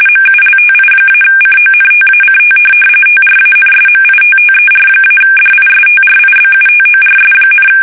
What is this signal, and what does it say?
RTTY, about 45.45 baud. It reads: MULTI-CONTACT TRANSMITTER-DISTRIBUTORS,
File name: fsk850 / teletype-corp